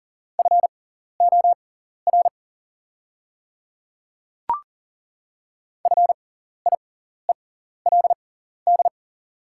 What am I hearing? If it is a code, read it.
Morse code, heard as a keyed tone, about 40 wpm, spaced as for 14 wpm, 700 Hz. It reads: FOR FIELD
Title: 6x Character Spacing - Top 500 Words - 40wpm